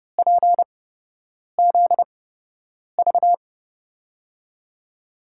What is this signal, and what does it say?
Morse code, heard as a keyed tone, about 31 wpm, spaced as for 8 wpm, 700 Hz. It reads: PZV
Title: USA Call Signs - 5 Characters - 8x Character Spacing - 30wpm